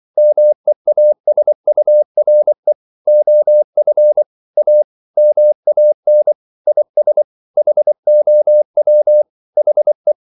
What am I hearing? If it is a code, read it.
Morse code, heard as a keyed tone, about 24 wpm, 600 Hz. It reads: MEASURE OF A MAN IS HOW HE
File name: CW / Practice / EnglishQuotations0224